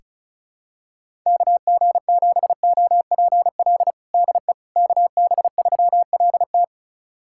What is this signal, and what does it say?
Morse code, heard as a keyed tone, about 34 wpm, 700 Hz. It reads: KG7OPL DE KB3LT
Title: USA Callsign DE Callsign 35wpm